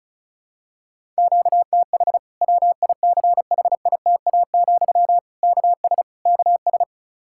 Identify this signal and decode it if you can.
Morse code, heard as a keyed tone, about 34 wpm, 700 Hz. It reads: QTH WICHITA, KS KS
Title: K1USN SST QSO Practice 35wpm